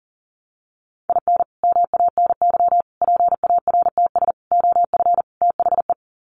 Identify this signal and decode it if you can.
Morse code, heard as a keyed tone, about 40 wpm, 700 Hz. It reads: IN MANY PARTS OF THE
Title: Sets of 5 Words Encore 40wpm